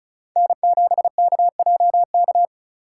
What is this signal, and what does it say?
Morse code, heard as a keyed tone, about 36 wpm, 700 Hz. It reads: N7KJK
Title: USA Call Signs - 6x Character Spacing - 35wpm